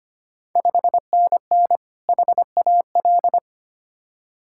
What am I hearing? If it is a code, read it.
Morse code, heard as a keyed tone, about 25 wpm, 690 Hz. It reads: 5NN HAL
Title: Ontario QSO Party Exchange 25wpm